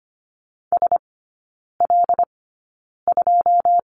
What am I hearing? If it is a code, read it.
Morse code, heard as a keyed tone, about 25 wpm, spaced as for 9 wpm, 690 Hz. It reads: SL2
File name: Lesson 14e - Number 2 - Three Characters 25wpm